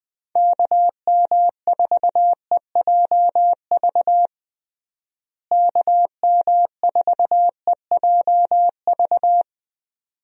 Morse code, heard as a keyed tone, about 21 wpm, 700 Hz. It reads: KM4EJV KM4EJV
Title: USA Call Signs - 6 Characters - Repeated 2x - 20wpm